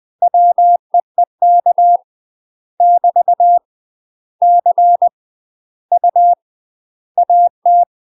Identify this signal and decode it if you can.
Morse code, heard as a keyed tone, about 20 wpm, 700 Hz. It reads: WEEK <BT> C U AT
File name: UK MASTER QSOs TRAINING / 20WPM / UK MASTER QSOs 0025